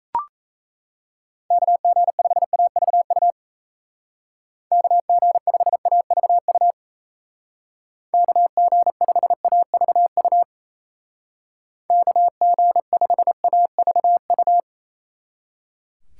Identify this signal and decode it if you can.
Morse code, heard as a keyed tone, about 33 wpm, 700 Hz. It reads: KG5AVU KG5AVU KG5AVU KG5AVU
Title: Speed Racing - USA 2x3 Call Signs 28wpm